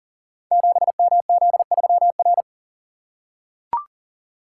Morse code, heard as a keyed tone, about 38 wpm, 700 Hz. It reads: 7MZ3R
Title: Code Groups - 5 Letter:Numbers - 40wpm